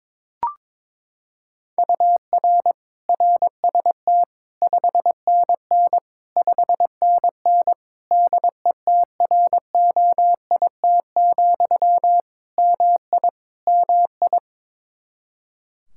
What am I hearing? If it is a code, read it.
Morse code, heard as a keyed tone, about 22 wpm, 700 Hz. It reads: UR RST 5NN 5NN DETROIT, MI MI
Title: Practice QSO - Exchange 1 - Beginning - 22wpm